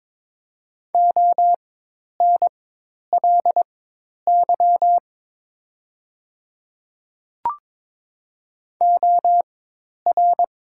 Morse code, heard as a keyed tone, about 22 wpm, spaced as for 10 wpm, 700 Hz. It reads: ONLY OR
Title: 4x Character Spacing - Top 300 Words - 22wpm